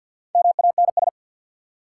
Morse code, heard as a keyed tone, about 52 wpm, 700 Hz. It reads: MANS
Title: Sentences from Top 1000 Words - Review Part 6 - New Word Drill - 50wpm